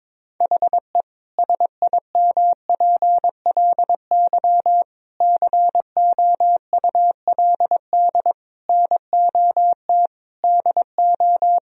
Morse code, heard as a keyed tone, about 22 wpm, 700 Hz. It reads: HE SIMPLY COULD NOT DO
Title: Sentences from Top 1000 Words - Review Part 4 - 22wpm